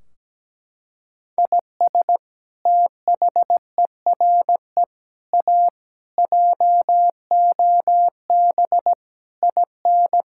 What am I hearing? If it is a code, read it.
Morse code, heard as a keyed tone, about 17 wpm, 700 Hz. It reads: IS THERE A JOB IN
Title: Sentences from Top 1600 Words 17wpm